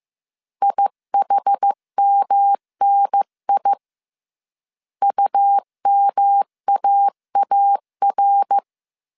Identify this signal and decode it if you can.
Morse code, heard as a keyed tone, about 15 wpm, 800 Hz. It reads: IHMNI UMAAR
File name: CW / DET / 1 3 1500C S C 14wpm 2458